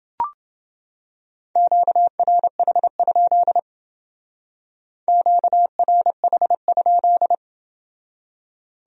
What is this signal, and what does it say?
Morse code, heard as a keyed tone, about 30 wpm, 700 Hz. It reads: QRH? QRH?